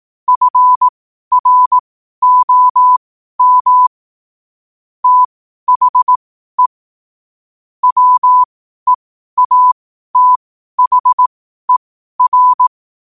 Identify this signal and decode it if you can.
Morse code, heard as a keyed tone, about 18 wpm, spaced as for 12 wpm, 1000 Hz. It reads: FROM THE WEATHER
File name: qotdfiles / 12wpm